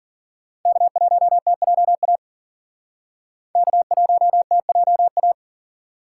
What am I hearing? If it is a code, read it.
Morse code, heard as a keyed tone, about 52 wpm, 700 Hz. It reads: K1TJA K1TJA